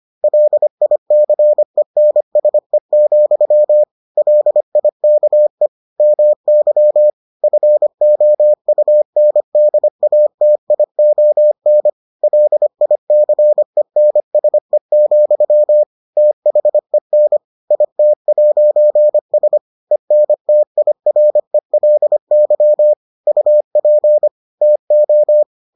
Morse code, heard as a keyed tone, about 25 wpm, 600 Hz. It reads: LICENSE, LIKE MY FOUNDATION LICENSE, THEN IT'S ENTIRELY UP TO